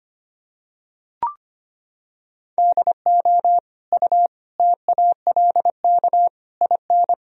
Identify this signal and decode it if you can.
Morse code, heard as a keyed tone, about 25 wpm, 700 Hz. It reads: DO U TALK IN
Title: Sentences from Top 500 Words and Abbreviations 25wpm